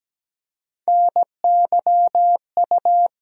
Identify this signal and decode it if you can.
Morse code, heard as a keyed tone, about 17 wpm, 700 Hz. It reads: NYU
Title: USA Call Sign Postfixes - 3 Letters 17wpm